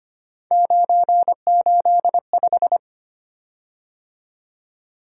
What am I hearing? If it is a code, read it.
Morse code, heard as a keyed tone, about 25 wpm, 700 Hz. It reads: 985
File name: Three Letter-Number Combinations - 25wpm